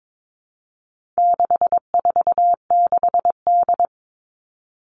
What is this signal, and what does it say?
Morse code, heard as a keyed tone, about 22 wpm, 700 Hz. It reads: 646D
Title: All Cut Numbers - 4-Digit 22wpm